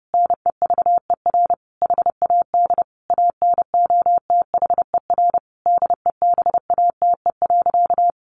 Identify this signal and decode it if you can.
Morse code, heard as a keyed tone, about 30 wpm, 700 Hz. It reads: NEVER HAD ANOTHER DEBATE.
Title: Sentences from Top 1100 Words 30wpm